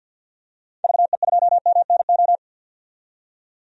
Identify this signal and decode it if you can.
Morse code, heard as a keyed tone, about 50 wpm, 700 Hz. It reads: VE2MNO
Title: DXCC-100 Call Signs - 6 Characters - 50wpm